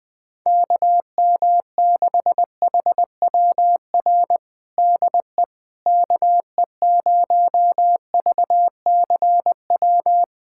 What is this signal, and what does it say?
Morse code, heard as a keyed tone, about 20 wpm, 700 Hz. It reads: KM6HWR DE KE0VCW
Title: USA Callsign DE Callsign - 6 Characters - 20wpm